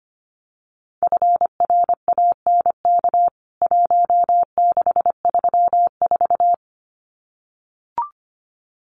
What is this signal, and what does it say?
Morse code, heard as a keyed tone, about 25 wpm, 700 Hz. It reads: FRANK 1634
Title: CWT Exchange 25wpm